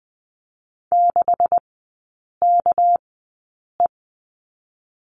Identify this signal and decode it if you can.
Morse code, heard as a keyed tone, about 20 wpm, spaced as for 8 wpm, 690 Hz. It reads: 6KE
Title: 212 Lesson 34e - Number 8 - Three Characters 20wpm